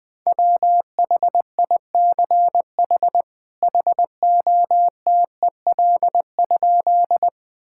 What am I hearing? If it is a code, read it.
Morse code, heard as a keyed tone, about 20 wpm, 700 Hz. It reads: WHICH HOTEL?
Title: Sentences from Top 1600 Words 20wpm